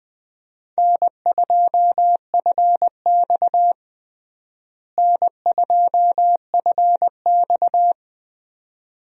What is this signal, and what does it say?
Morse code, heard as a keyed tone, about 21 wpm, 700 Hz. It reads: N2FX N2FX
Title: USA Call Signs - 4 Characters - Repeated 2x - 20wpm